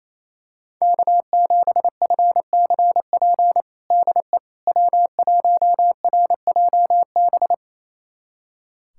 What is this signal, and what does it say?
Morse code, heard as a keyed tone, about 28 wpm, 700 Hz. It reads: K7FCP DE W1RJB
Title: USA Callsign DE Callsign - 5 Characters - 28wpm